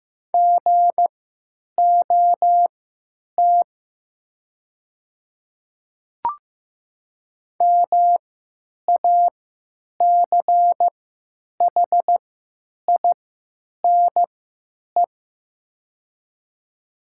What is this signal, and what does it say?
Morse code, heard as a keyed tone, about 16 wpm, spaced as for 9 wpm, 700 Hz. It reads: GOT MACHINE
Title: 3x Character Spacing - Top 500 Words - 15wpm